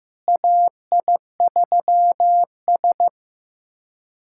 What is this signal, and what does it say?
Morse code, heard as a keyed tone, about 16 wpm, 700 Hz. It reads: AI3S
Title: USA Callsign DE Callsign - 4 Characters - 15wpm